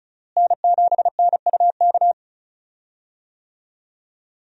Morse code, heard as a keyed tone, about 35 wpm, 700 Hz. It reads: N7NUK
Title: USA Call Signs - 5 Characters - Repeated 3x - 35wpm